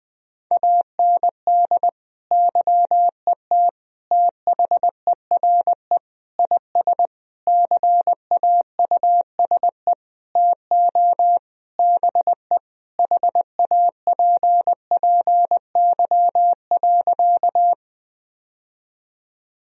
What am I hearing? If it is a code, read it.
Morse code, heard as a keyed tone, about 20 wpm, 700 Hz. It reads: AND YET THERE IS CAUSE TO BE HAPPY.
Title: Sentences from Top 1000 Words - Review Part 1 - 20wpm